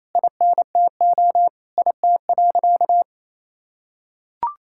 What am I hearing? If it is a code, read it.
Morse code, heard as a keyed tone, about 28 wpm, 700 Hz. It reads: INTO IT.
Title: Sentences from Top 1000 Words - Review Part 5 - 28wpm